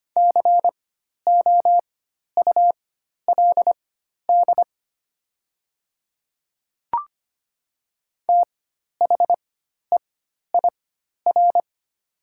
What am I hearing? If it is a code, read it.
Morse code, heard as a keyed tone, about 25 wpm, spaced as for 12 wpm, 700 Hz. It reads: COULD THEIR
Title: 4x Character Spacing - Top 100 Words - 25wpm